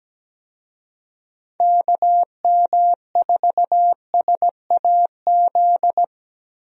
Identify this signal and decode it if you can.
Morse code, heard as a keyed tone, about 17 wpm, 700 Hz. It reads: KM4SAZ
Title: USA Call Signs - 6 Characters - Repeated 2x - 17wpm